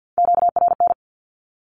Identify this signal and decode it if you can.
Morse code, heard as a keyed tone, about 48 wpm, 700 Hz. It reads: QRN
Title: Q-Codes 50wpm